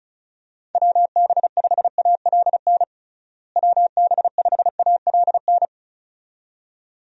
Morse code, heard as a keyed tone, about 34 wpm, 700 Hz. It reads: WB5ALN WB5ALN
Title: USA Call Signs - 6 Characters - Repeated 2x - 35wpm